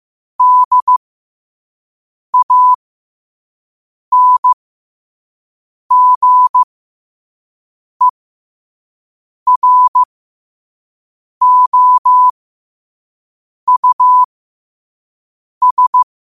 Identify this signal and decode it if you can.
Morse code, heard as a keyed tone, about 15 wpm, spaced as for 5 wpm, 1000 Hz. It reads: DANGEROUS